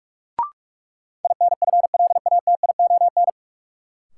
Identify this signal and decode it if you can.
Morse code, heard as a keyed tone, about 46 wpm, 700 Hz. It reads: INFLATION
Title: Sentences from Top 1100 Words - New Word Drill Alt - 45wpm